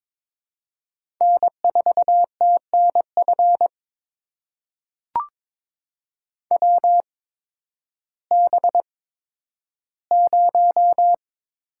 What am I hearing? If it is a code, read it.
Morse code, heard as a keyed tone, about 22 wpm, spaced as for 12 wpm, 700 Hz. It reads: N4TNF WB0
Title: USA Call Signs - 8x Character Spacing - 22wpm